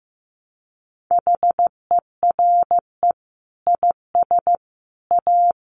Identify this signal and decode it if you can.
Morse code, heard as a keyed tone, about 15 wpm, 700 Hz. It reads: HERE IS A
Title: Sentences from Top 1100 Words 15wpm